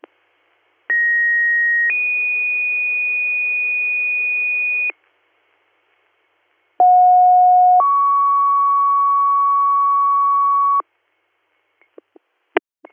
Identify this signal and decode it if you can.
Two-tone page, heard as a one-second tone, then a three-second tone, about 1900 then 2450 Hz, about 730 then 1120 Hz. It is heard from 0.9 to 10.8 s.